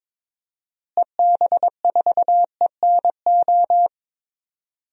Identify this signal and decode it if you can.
Morse code, heard as a keyed tone, about 22 wpm, 700 Hz. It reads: EB4ENO
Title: DXCC-100 Call Signs - 6 Characters - Repeated 2x - 22wpm